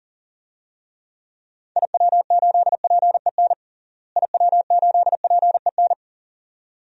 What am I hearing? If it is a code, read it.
Morse code, heard as a keyed tone, about 37 wpm, 700 Hz. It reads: IW8PEN IW8PEN
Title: DXCC-100 Call Signs - 6 Characters - Repeated 2x - 40wpm